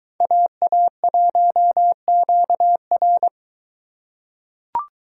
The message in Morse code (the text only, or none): AA1QR